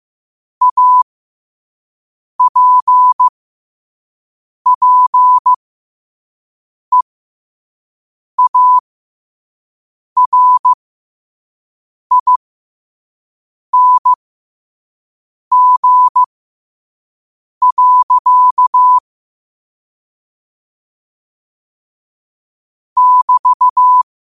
Morse code, heard as a keyed tone, about 15 wpm, spaced as for 5 wpm, 1000 Hz. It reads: APPEARING. <BT>
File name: qotdfiles / 5wpm